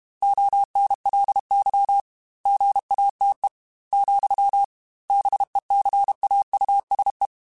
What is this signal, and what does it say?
Morse code, heard as a keyed tone, about 32 wpm, 800 Hz. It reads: ONLY GATE, BECAUSE
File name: Technical Subjects / Technical Subjects-mp3s / AND Gate @30